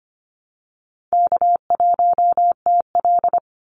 Morse code, heard as a keyed tone, about 25 wpm, 700 Hz. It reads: K1TL
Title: Call Signs - 25wpm